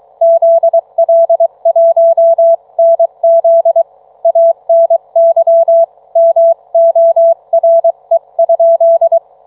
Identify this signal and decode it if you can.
Morse code, heard as a keyed tone, about 23 wpm, 660 Hz. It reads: ZL1NZ ANY MORE?